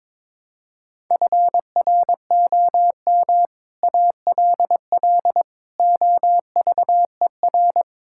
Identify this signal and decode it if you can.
Morse code, heard as a keyed tone, about 22 wpm, 700 Hz. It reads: FROM ALL OVER
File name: practice / 22wpm / Build A Phrase - 5-Word Phrases - 22wpm